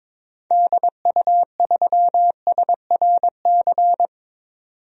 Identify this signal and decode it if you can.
Morse code, heard as a keyed tone, about 22 wpm, 700 Hz. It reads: DU3SRC
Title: DXCC-100 Call Signs - 6 Characters - 22wpm